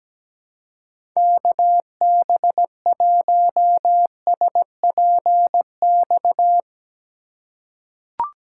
Morse code, heard as a keyed tone, about 17 wpm, 700 Hz. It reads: KB1SPX
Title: USA Call Signs - 2x Character Spacing - 17wpm